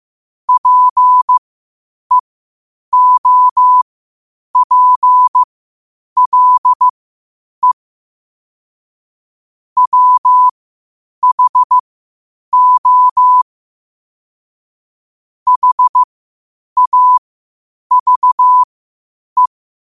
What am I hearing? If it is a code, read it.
Morse code, heard as a keyed tone, about 15 wpm, spaced as for 8 wpm, 1000 Hz. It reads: PEOPLE WHO HAVE
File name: qotdfiles / 8wpm